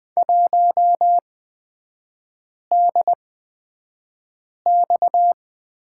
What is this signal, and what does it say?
Morse code, heard as a keyed tone, about 20 wpm, spaced as for 5 wpm, 700 Hz. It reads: 1DX
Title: 229 Lesson 37e - Letter x - Three Characters 20wpm